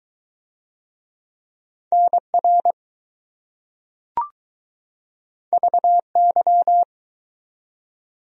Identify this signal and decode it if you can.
Morse code, heard as a keyed tone, about 20 wpm, 700 Hz. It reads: NR VY